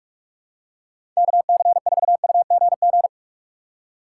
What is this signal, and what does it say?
Morse code, heard as a keyed tone, about 46 wpm, 700 Hz. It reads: KC4UGG
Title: USA Call Signs - 6 Characters - 4x Character Spacing - 45wpm